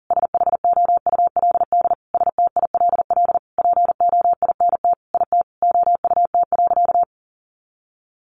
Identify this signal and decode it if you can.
Morse code, heard as a keyed tone, about 40 wpm, 700 Hz. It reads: SHOULD STILL POINT IT OUT.
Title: Sentences from Top 1500 Words 40wpm